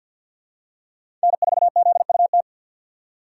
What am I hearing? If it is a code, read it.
Morse code, heard as a keyed tone, about 53 wpm, 700 Hz. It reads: N4ZUT
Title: USA Call Signs - 5 Characters - 2x Character Spacing - 50wpm